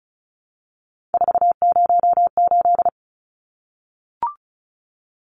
Morse code, heard as a keyed tone, about 34 wpm, 700 Hz. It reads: 408
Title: Numbers - 3-Digit 35wpm